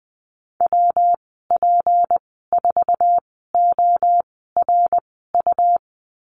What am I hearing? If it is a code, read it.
Morse code, heard as a keyed tone, about 20 wpm, spaced as for 15 wpm, 700 Hz. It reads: WP4ORU
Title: 154 Lesson 25g - Letter p - Call Signs 20wpm